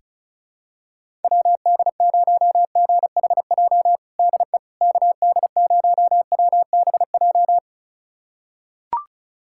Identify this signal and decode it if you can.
Morse code, heard as a keyed tone, about 36 wpm, 700 Hz. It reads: WD0GHJ DE KD0WBJ
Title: USA Callsign DE Callsign - 6 Characters - 35wpm